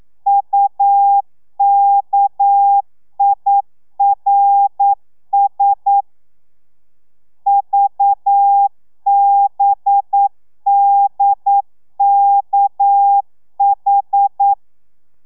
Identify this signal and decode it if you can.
Morse code, heard as a keyed tone, about 9 wpm, 800 Hz. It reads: UKIRS VBDKH
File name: telegrafia / i2tfj / Lez6 40